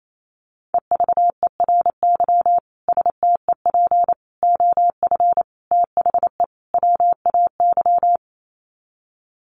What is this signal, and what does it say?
Morse code, heard as a keyed tone, about 28 wpm, 700 Hz. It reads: EVERY STEP OF THE WAY
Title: Sets of 5 Words Encore2 28wpm